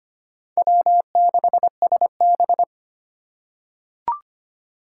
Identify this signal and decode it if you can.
Morse code, heard as a keyed tone, about 25 wpm, 700 Hz. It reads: W6SB
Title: USA Call Signs - 4 Characters - 3x Character Spacing - 25wpm